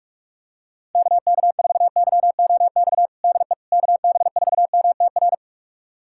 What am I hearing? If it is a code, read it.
Morse code, heard as a keyed tone, about 48 wpm, 700 Hz. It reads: KK4YOX DE KB4MTR
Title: USA Callsign DE Callsign 45wpm